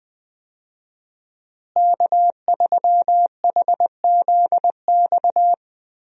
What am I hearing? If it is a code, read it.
Morse code, heard as a keyed tone, about 20 wpm, 700 Hz. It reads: K3HZX
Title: USA Call Signs - 5 Characters - Repeated 2x - 20wpm